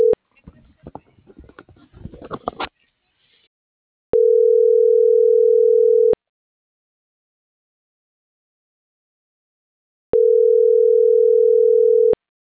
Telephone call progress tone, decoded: ringback tone

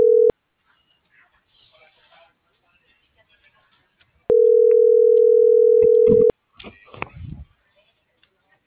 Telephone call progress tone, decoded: ringback tone

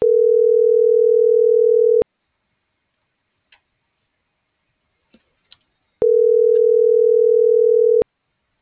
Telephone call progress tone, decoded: ringback tone

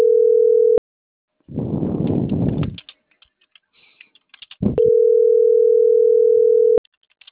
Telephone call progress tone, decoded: ringback tone

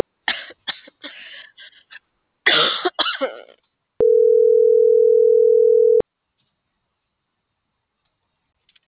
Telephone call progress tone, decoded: ringback tone

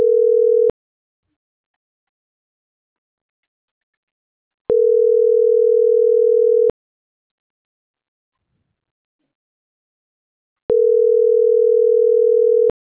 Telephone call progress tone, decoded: ringback tone

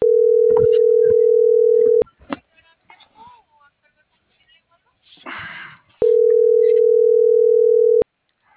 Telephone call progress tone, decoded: ringback tone

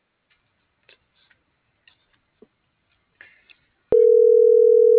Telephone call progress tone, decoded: ringback tone

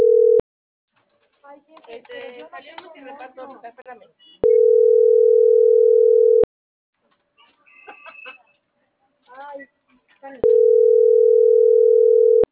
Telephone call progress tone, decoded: ringback tone

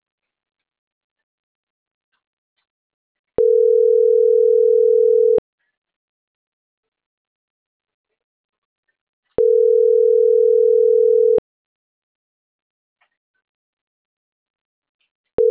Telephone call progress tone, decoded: ringback tone

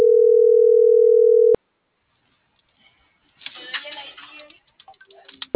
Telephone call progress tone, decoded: ringback tone

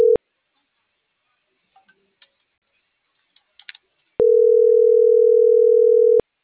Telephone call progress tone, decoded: ringback tone

